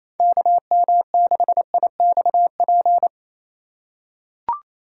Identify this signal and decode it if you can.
Morse code, heard as a keyed tone, about 28 wpm, 700 Hz. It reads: KM6IXP
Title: USA Call Signs - 6x Character Spacing - 28wpm